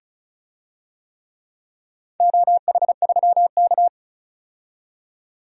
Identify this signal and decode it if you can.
Morse code, heard as a keyed tone, about 36 wpm, 700 Hz. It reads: OH3K
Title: DXCC-100 Call Signs - 4 Characters - 35wpm